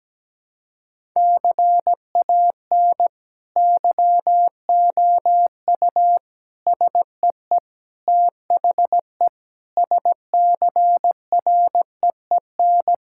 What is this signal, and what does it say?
Morse code, heard as a keyed tone, about 17 wpm, 700 Hz. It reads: CAN YOU SEE THE SCREEN
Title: Sentences from Top 1000 Words - Review Part 6 - 17wpm